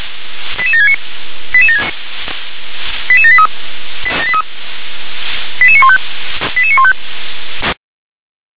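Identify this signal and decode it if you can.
Five-tone selective call (ZVEI), heard as quick runs of five tones, tones of about 70 ms. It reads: 88769 88783 88915 88915
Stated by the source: ZVEI five-tone call received by SDR.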